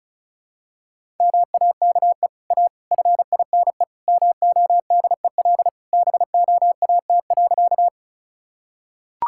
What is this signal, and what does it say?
Morse code, heard as a keyed tone, about 36 wpm, 700 Hz. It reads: MAKE A FINE MODEL BOAT.
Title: Sentences from Top 1000 Words - Review Part 5 - 35wpm